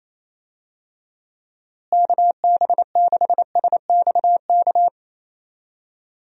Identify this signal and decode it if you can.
Morse code, heard as a keyed tone, about 27 wpm, 700 Hz. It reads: KB6SXK
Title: USA Call Signs - 6 Characters - Repeated 2x - 28wpm